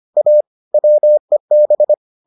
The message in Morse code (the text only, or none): A WEB